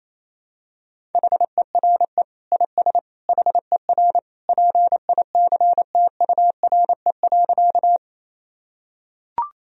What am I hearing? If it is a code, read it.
Morse code, heard as a keyed tone, about 28 wpm, 700 Hz. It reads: HERE IS HER PICTURE.
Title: Sentences from Top 1000 Words - Review Part 1 - 28wpm